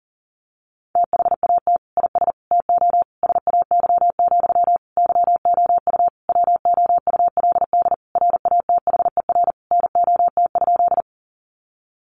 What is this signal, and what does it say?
Morse code, heard as a keyed tone, about 40 wpm, 700 Hz. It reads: THAT IS TO SAY, YOU WOULD RATHER NOT?
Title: Sentences from Top 700 Words 40wpm